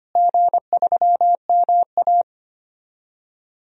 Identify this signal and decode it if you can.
Morse code, heard as a keyed tone, about 25 wpm, 700 Hz. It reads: G3MA
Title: DXCC-100 Call Signs - 4 Characters - Repeated 2x - 25wpm